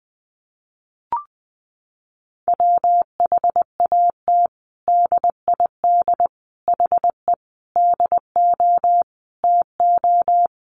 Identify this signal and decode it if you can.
Morse code, heard as a keyed tone, about 20 wpm, 700 Hz. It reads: WHAT DID HE DO TO
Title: Sentences from Top 100 Words 20wpm